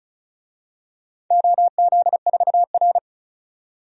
Morse code, heard as a keyed tone, about 36 wpm, 700 Hz. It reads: OZ4R
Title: DXCC-100 Call Signs - 4 Characters - 35wpm